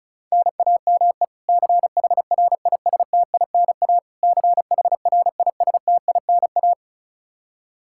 Morse code, heard as a keyed tone, about 36 wpm, 700 Hz. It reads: NAME CHRISTINA CHRISTINA
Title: K1USN SST QSO Practice 35wpm